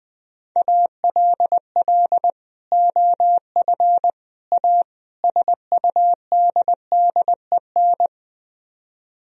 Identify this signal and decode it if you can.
Morse code, heard as a keyed tone, about 20 wpm, 700 Hz. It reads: ALL OF A SUDDEN
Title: Speed-Racing - Sets of 4 Words 20wpm